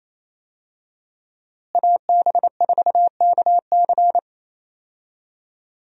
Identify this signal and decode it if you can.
Morse code, heard as a keyed tone, about 28 wpm, 700 Hz. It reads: AB4KC
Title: USA Call Signs - 5 Characters - Repeated 2x - 28wpm